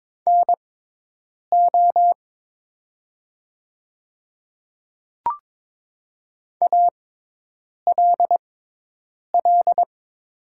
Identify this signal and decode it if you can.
Morse code, heard as a keyed tone, about 22 wpm, spaced as for 8 wpm, 700 Hz. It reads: NO ALL
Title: 6x Character Spacing - Top 100 Words - 22wpm